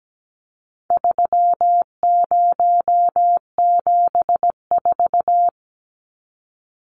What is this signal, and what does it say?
Morse code, heard as a keyed tone, about 17 wpm, 700 Hz. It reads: 3074
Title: Numbers - 4-Digit 17wpm